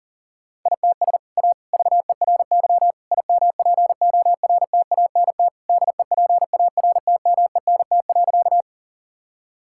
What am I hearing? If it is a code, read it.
Morse code, heard as a keyed tone, about 37 wpm, 700 Hz. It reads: ITS A VERY IMPORTANT DEPARTMENT.
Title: Sentences from Top 1000 Words - Review Part 5 - 40wpm